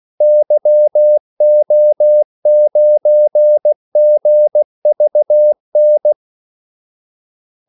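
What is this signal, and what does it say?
Morse code, heard as a keyed tone, about 16 wpm, 600 Hz. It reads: YO9GVN
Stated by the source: Morse call sign practice at 16 wpm.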